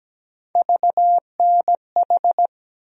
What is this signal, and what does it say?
Morse code, heard as a keyed tone, about 17 wpm, 700 Hz. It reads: VNH
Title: USA Call Sign Postfixes - 3 Letters 17wpm